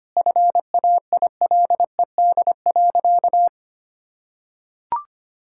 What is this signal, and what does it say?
Morse code, heard as a keyed tone, about 25 wpm, 700 Hz. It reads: FAILED.
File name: practice / 25wpm / Sentences with 9-Word Phrases - 25wpm